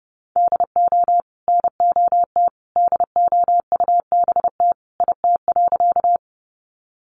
Morse code, heard as a keyed tone, about 30 wpm, 700 Hz. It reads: DO NOT DOUBT IT.
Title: Sentences from Top 1500 Words 30wpm